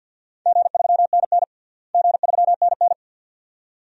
Morse code, heard as a keyed tone, about 52 wpm, 700 Hz. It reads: G3NN G3NN
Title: DXCC-100 Call Signs - 4 Characters - Repeated 2x - 50wpm